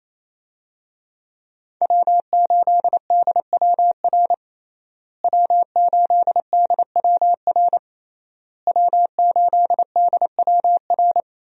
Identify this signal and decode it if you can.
Morse code, heard as a keyed tone, about 28 wpm, 700 Hz. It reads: W8DWR W8DWR W8DWR
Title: USA Call Signs - 5 Characters - Repeated 3x - 28wpm